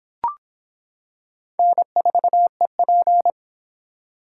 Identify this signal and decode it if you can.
Morse code, heard as a keyed tone, about 25 wpm, 700 Hz. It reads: N4EP